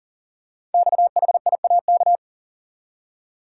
Morse code, heard as a keyed tone, about 42 wpm, 700 Hz. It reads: XHIAK
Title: Code Groups - 5 Letter:Numbers - 40wpm